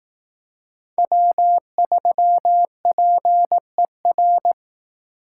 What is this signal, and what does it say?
Morse code, heard as a keyed tone, about 18 wpm, 700 Hz. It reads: W3PER
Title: Speed Racing - USA 1x3 Call Signs 15wpm